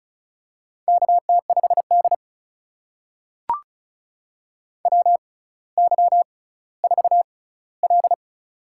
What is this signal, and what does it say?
Morse code, heard as a keyed tone, about 36 wpm, spaced as for 18 wpm, 700 Hz. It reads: KT5D WY4L
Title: USA Call Signs - 6x Character Spacing - 35wpm